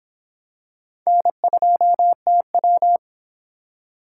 Morse code, heard as a keyed tone, about 24 wpm, 700 Hz. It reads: N2TW